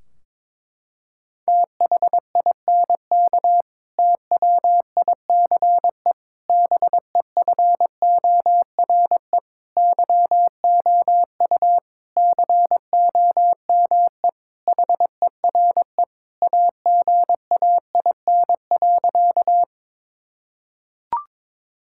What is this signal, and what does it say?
Morse code, heard as a keyed tone, about 22 wpm, 700 Hz. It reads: THINK TWICE BEFORE YOU COME HERE AGAIN.
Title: Sentences from Top 1600 Words 22wpm